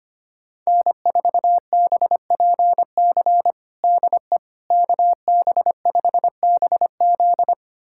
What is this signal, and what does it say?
Morse code, heard as a keyed tone, about 25 wpm, 700 Hz. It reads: N4BPC DE KB5BZ
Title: USA Callsign DE Callsign - 5 Characters - 25wpm